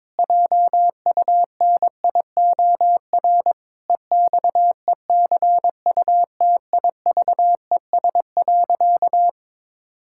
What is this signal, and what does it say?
Morse code, heard as a keyed tone, about 22 wpm, 700 Hz. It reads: JUNIOR EXECUTIVES.
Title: Pangram Challenge v2 - 22wpm